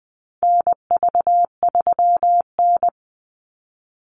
Morse code, heard as a keyed tone, about 20 wpm, 700 Hz. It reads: NV3N
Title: Call Signs Repeated 2x - 20wpm